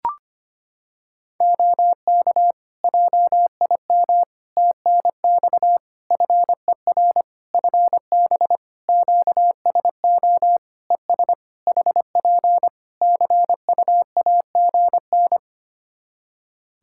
Morse code, heard as a keyed tone, about 25 wpm, 700 Hz. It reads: OK JIM TNX FER FB QSO ES HP CUAGN
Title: Practice QSO - Exchange 4 - Ending - 25wpm